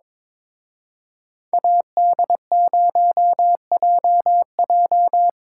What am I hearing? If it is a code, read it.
Morse code, heard as a keyed tone, about 25 wpm, 700 Hz. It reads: AD0JJ
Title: Speed Racing - USA Call Signs 22wpm